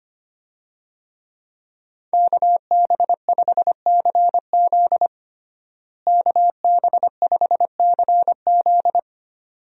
Morse code, heard as a keyed tone, about 25 wpm, 700 Hz. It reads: KB5CZ KB5CZ
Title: USA Call Signs - 5 Characters - Repeated 2x - 25wpm